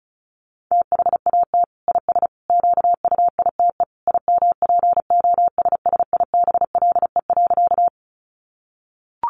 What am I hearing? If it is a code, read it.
Morse code, heard as a keyed tone, about 34 wpm, 700 Hz. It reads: THAT IS QUITE IMPOSSIBLE.
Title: Sentences from Top 1100 Words 35wpm